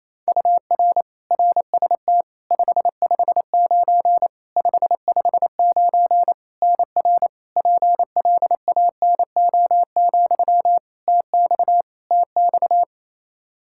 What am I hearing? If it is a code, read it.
Morse code, heard as a keyed tone, about 28 wpm, 700 Hz. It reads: UR RST 559 559 NR PLANO, TX TX
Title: Practice QSO - Exchange 1 - Beginning - 28wpm